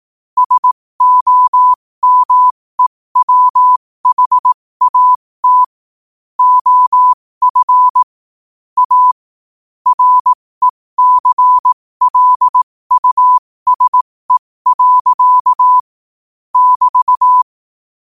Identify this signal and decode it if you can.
Morse code, heard as a keyed tone, about 18 wpm, spaced as for 15 wpm, 1000 Hz. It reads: SOMEWHAT OF A RECLUSE. <BT>
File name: qotdfiles / 15wpm